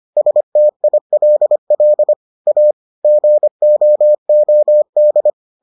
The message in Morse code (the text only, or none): STILL A GOOD